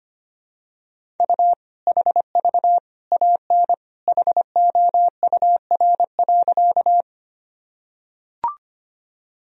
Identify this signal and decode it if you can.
Morse code, heard as a keyed tone, about 25 wpm, 700 Hz. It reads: U HV AN HOUR.
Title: Sentences from Top 400 Words and Abbreviations 25wpm